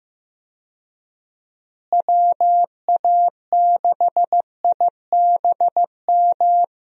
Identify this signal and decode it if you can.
Morse code, heard as a keyed tone, about 15 wpm, 700 Hz. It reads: WA6IBM
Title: USA Call Signs - 6 Characters - Repeated 2x - 15wpm